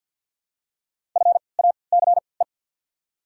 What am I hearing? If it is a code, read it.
Morse code, heard as a keyed tone, about 48 wpm, spaced as for 28 wpm, 700 Hz. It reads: FACE
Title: 3x Character Spacing - Top 500 Words - 50wpm